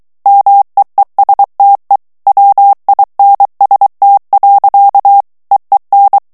Morse code, heard as a keyed tone, about 23 wpm, 800 Hz. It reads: MEESTE WINST. EEN
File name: HOLLANDAIS texte clair 1 vitesse 22 mots minute